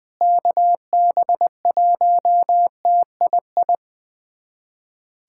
Morse code, heard as a keyed tone, about 21 wpm, 700 Hz. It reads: KB1TII
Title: USA Callsign DE Callsign 20wpm